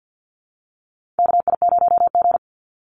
Morse code, heard as a keyed tone, about 50 wpm, 700 Hz. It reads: KI0G
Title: NA SOTA Chaser Callsigns 50wpm